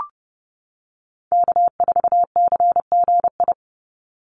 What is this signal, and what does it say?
Morse code, heard as a keyed tone, about 30 wpm, 700 Hz. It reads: K4CGI
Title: USA Call Signs 30wpm